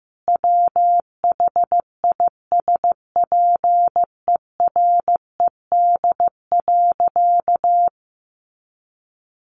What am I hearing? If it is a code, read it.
Morse code, heard as a keyed tone, about 15 wpm, 700 Hz. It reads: WHISPERED.
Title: A Sound Of Thunder 15wpm